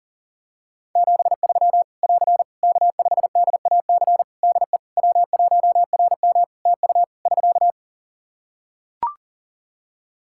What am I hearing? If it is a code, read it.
Morse code, heard as a keyed tone, about 40 wpm, 700 Hz. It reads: 73 <AR> K5DAC DE W1RM TU <SK>
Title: Practice QSO - Exchange 4 - Ending - 40wpm